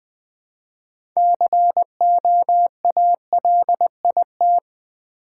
Morse code, heard as a keyed tone, about 20 wpm, 700 Hz. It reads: COALIT
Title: Build A Word - 3 Letters At A Time - 9-Letter Words - 20wpm